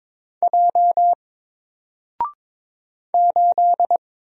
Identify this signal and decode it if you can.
Morse code, heard as a keyed tone, about 22 wpm, 700 Hz. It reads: J 8
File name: Single-Letter-Number-22wpm